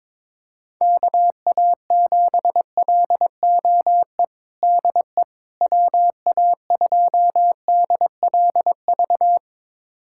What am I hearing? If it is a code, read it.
Morse code, heard as a keyed tone, about 22 wpm, 700 Hz. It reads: KA7LOE DE WA2DLV
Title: USA Callsign DE Callsign - 6 Characters - 22wpm